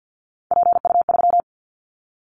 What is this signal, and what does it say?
Morse code, heard as a keyed tone, about 50 wpm, 700 Hz. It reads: PU3
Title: Callsign Prefixes for Top 50 DXCC - 50wpm